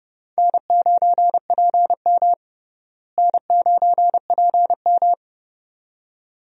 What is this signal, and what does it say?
Morse code, heard as a keyed tone, about 30 wpm, 700 Hz. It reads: N9PM N9PM
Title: USA Call Signs - 4 Characters - Repeated 2x - 30wpm